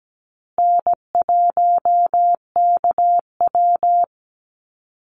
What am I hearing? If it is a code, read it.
Morse code, heard as a keyed tone, about 17 wpm, 700 Hz. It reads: N1KW
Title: Call Signs Repeated 2x - 17wpm